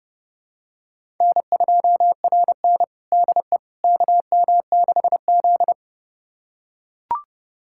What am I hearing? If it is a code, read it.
Morse code, heard as a keyed tone, about 30 wpm, 700 Hz. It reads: N2RN DE KM6Z
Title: USA Callsign DE Callsign - 4 Characters - 30wpm